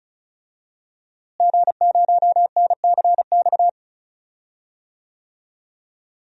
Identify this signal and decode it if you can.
Morse code, heard as a keyed tone, about 34 wpm, 700 Hz. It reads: G0NCX
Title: DXCC-100 Call Signs - 5 Characters - Repeated 3x - 35wpm